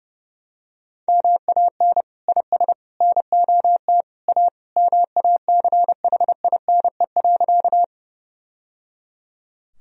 Morse code, heard as a keyed tone, about 30 wpm, 700 Hz. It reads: MAN IS NOT A MACHINE.
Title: Sentences from Top 1000 Words - Review Part 2 - 30wpm